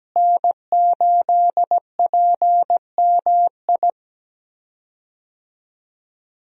Morse code, heard as a keyed tone, about 17 wpm, 700 Hz. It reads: N8PMI
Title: USA Call Signs - 5 Characters - Repeated 3x - 17wpm